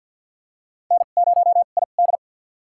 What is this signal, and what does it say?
Morse code, heard as a keyed tone, about 49 wpm, spaced as for 36 wpm, 690 Hz. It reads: N0ID
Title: USA Call Signs - 2x Character Spacing - 50wpm